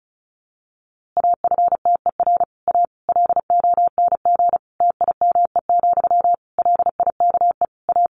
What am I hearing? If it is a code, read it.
Morse code, heard as a keyed tone, about 34 wpm, 700 Hz. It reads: AFTER A LONG TIME, LIKE A
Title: A Sound Of Thunder 35wpm